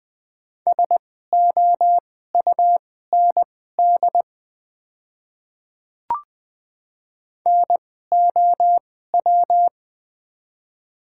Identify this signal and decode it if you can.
Morse code, heard as a keyed tone, about 20 wpm, spaced as for 14 wpm, 700 Hz. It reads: SOUND NOW
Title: 2x Character Spacing - Top 100 Words - 20wpm